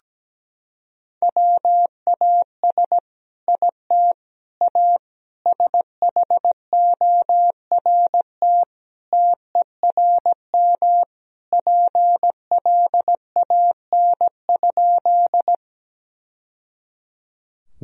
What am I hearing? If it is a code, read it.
Morse code, heard as a keyed tone, about 17 wpm, 700 Hz. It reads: WAS IT A SHORT TERM PLAN?
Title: Sentences from Top 1000 Words - Review Part 4 - 17wpm